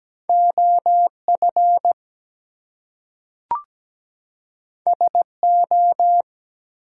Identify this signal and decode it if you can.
Morse code, heard as a keyed tone, about 17 wpm, 700 Hz. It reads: OF SO
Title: Two Letter Words - 17wpm